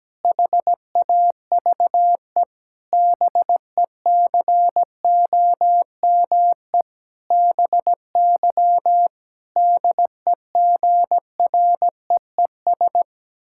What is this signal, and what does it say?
Morse code, heard as a keyed tone, about 17 wpm, 700 Hz. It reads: HAVE BECOME BY DEGREES
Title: Pangram Challenge v2 - 17wpm